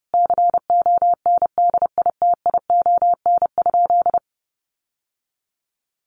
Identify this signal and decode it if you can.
Morse code, heard as a keyed tone, about 28 wpm, 700 Hz. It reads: CONDITION?
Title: Sentences from Top 1300 Words 30wpm